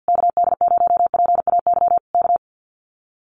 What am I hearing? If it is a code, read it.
Morse code, heard as a keyed tone, about 50 wpm, 700 Hz. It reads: KD0PAY K
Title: Common QSO Phrases Encore 50wpm